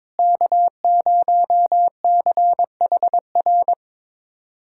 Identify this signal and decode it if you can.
Morse code, heard as a keyed tone, about 22 wpm, 700 Hz. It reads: K0CHR
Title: USA Callsign DE Callsign - 5 Characters - 22wpm